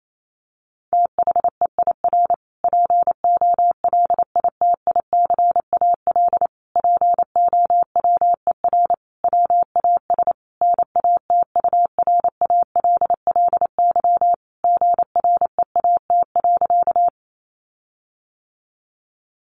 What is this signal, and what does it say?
Morse code, heard as a keyed tone, about 28 wpm, 700 Hz. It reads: THEIR POLITICAL POWER WAS NATURALLY GREAT.
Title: Sentences from Top 1300 Words 28wpm